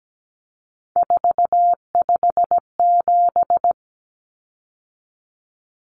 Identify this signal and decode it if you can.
Morse code, heard as a keyed tone, about 17 wpm, 690 Hz. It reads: 457
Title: Common Cut Numbers - 3-Digit 17wpm